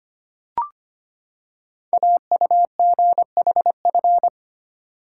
Morse code, heard as a keyed tone, about 25 wpm, 700 Hz. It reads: AUGHF
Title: Code Groups - 5 Letters - 25wpm